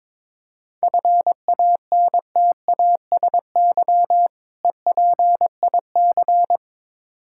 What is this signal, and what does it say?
Morse code, heard as a keyed tone, about 22 wpm, 690 Hz. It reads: FANTASY EPIC
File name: Difficult Words with Hints - 22wpm